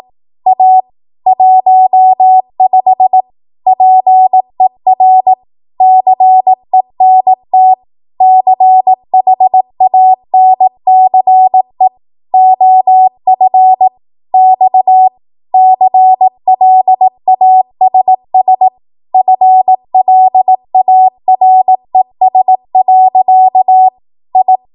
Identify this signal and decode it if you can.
Morse code, heard as a keyed tone, about 18 wpm, 750 Hz. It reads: A 15 PER CENT CHANCE OF X CLASS FLARES. I
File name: Archive / 18 WPM / 18WPM